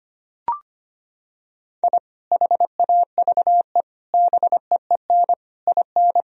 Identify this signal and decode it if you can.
Morse code, heard as a keyed tone, about 25 wpm, 700 Hz. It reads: I HAVE BEEN IN
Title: Sentences from Top 1000 Words - Review Part 3 - 25wpm